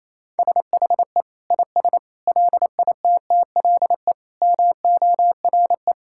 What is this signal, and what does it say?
Morse code, heard as a keyed tone, about 28 wpm, 700 Hz. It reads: SHE IS LITTLE MORE
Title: Sentences from Top 1600 Words 28wpm